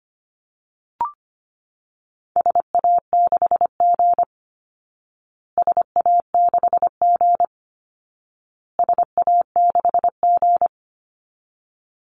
Morse code, heard as a keyed tone, about 25 wpm, 700 Hz. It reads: SA6G SA6G SA6G
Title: Call Signs Repeated 2x - 25wpm